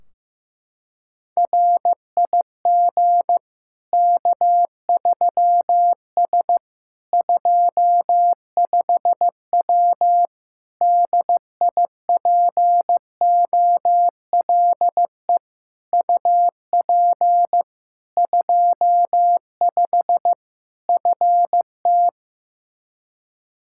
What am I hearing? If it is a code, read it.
Morse code, heard as a keyed tone, about 15 wpm, 700 Hz. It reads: RIG K3S 25W DIPOLE UP 25 FT
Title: K1USN SST QSO Practice 15wpm